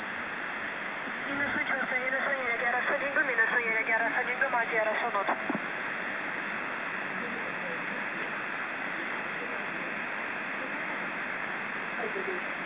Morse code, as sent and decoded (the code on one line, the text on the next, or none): none